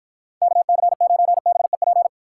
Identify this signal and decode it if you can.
Morse code, heard as a keyed tone, about 53 wpm, 700 Hz. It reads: KC9BEP